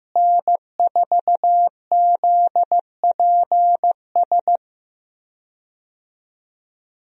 Morse code, heard as a keyed tone, about 15 wpm, 700 Hz. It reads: N4ZPS
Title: USA Call Signs - 5 Characters - Repeated 3x - 15wpm